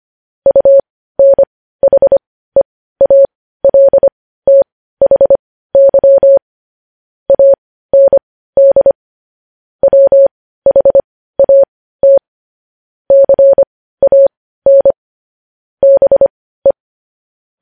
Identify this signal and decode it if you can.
Morse code, heard as a keyed tone, about 25 wpm, spaced as for 15 wpm, 550 Hz. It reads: UNHEALTHY AND WHAT CAN BE
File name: podcast / News headlines-1x-25x15